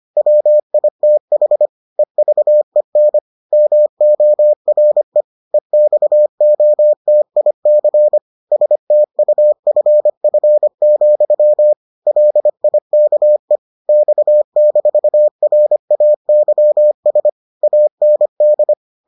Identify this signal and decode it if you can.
Morse code, heard as a keyed tone, about 25 wpm, 600 Hz. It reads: WITH EVEN MORE EXOTIC STUFF, LIKE X-RAYS AND